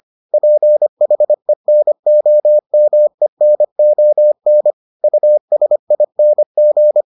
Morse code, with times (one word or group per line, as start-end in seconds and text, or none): 0.3-4.7 PHENOMENON
5.0-7.0 USING